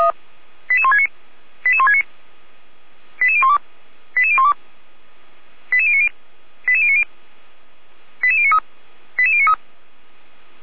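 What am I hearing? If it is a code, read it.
Five-tone selective call (ZVEI), heard as quick runs of five tones, tones of about 70 ms. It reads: 88169 88169 88012 88012 88090 88090 88093 88093
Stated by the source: Received on an SDR; ZVEI code 88199, from the Feuerwehr Orlamünde fire brigade server.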